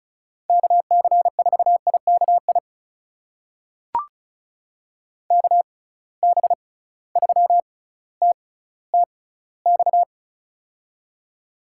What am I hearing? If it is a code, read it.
Morse code, heard as a keyed tone, about 36 wpm, spaced as for 18 wpm, 700 Hz. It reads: KC4IKI KB3TTX
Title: USA Call Signs - 6 Characters - 6x Character Spacing - 35wpm